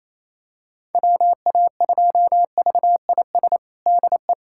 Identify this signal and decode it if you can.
Morse code, heard as a keyed tone, about 28 wpm, 700 Hz. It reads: WA2VIS DE
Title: USA Callsign DE Callsign - 6 Characters - 28wpm